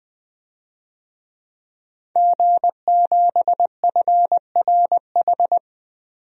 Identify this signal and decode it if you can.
Morse code, heard as a keyed tone, about 21 wpm, 700 Hz. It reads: G7FRH
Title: DXCC-100 Call Signs - 5 Characters - 20wpm